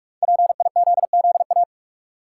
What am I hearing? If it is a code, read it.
Morse code, heard as a keyed tone, about 46 wpm, 700 Hz. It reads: PIZZA
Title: Five Letter Words - 45wpm